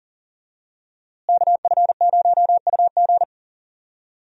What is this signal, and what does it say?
Morse code, heard as a keyed tone, about 40 wpm, 690 Hz. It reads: KF0UG
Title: USA Call Signs - 5 Characters - Repeated 3x - 40wpm